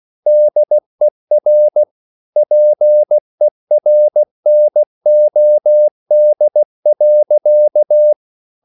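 Morse code, heard as a keyed tone, about 16 wpm, 600 Hz. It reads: DER PERNOD.
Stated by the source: Slow (QRS) Morse code at 16 wpm, from CW med Gnister.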